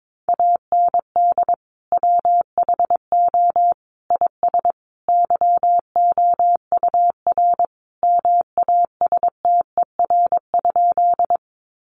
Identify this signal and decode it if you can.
Morse code, heard as a keyed tone, about 22 wpm, 700 Hz. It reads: AND WHO IS YOUR MASTER?
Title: Sentences from Top 1300 Words 22wpm